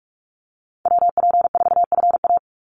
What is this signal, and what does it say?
Morse code, heard as a keyed tone, about 44 wpm, 700 Hz. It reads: WP4FA
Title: USA Call Signs 45wpm